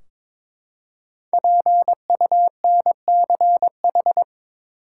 Morse code, heard as a keyed tone, about 22 wpm, 700 Hz. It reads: PUNCH
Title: Five Letter Words - 22wpm